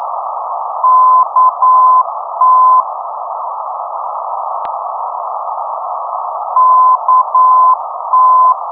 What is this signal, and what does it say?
Morse code, heard as a keyed tone, about 9 wpm, 1020 Hz, with noise. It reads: KT KT
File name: ndbs